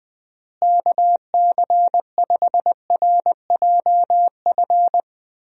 Morse code, heard as a keyed tone, about 20 wpm, 700 Hz. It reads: KC5RJF
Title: USA Call Signs - 6 Characters - Repeated 2x - 20wpm